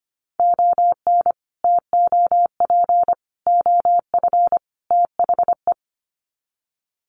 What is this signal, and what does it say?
Morse code, heard as a keyed tone, about 25 wpm, 700 Hz. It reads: ON TOP OF THE
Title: Sets of 4 Words 25wpm